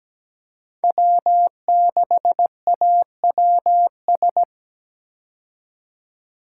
Morse code, heard as a keyed tone, about 17 wpm, 700 Hz. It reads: W6AWS
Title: K1USN SST QSO Practice - 17wpm